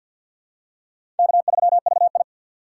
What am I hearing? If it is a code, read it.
Morse code, heard as a keyed tone, about 50 wpm, 700 Hz. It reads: K3VI
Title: USA Call Signs - 4 Characters - 3x Character Spacing - 50wpm